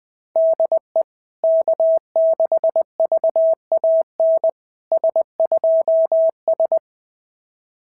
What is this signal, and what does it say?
Morse code, heard as a keyed tone, about 20 wpm, 650 Hz. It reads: DE K6VAN S2S
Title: SOTA QSO Practice 20wpm